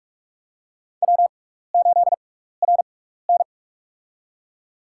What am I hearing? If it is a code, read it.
Morse code, heard as a keyed tone, about 47 wpm, spaced as for 16 wpm, 700 Hz. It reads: W8RN
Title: USA Call Signs - 6x Character Spacing - 45wpm